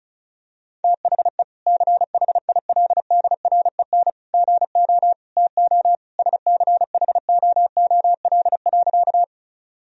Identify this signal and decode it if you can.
Morse code, heard as a keyed tone, about 36 wpm, 700 Hz. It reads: THE CHILDREN GO TO SCHOOL.
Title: Sentences from Top 1000 Words - Review Part 1 - 35wpm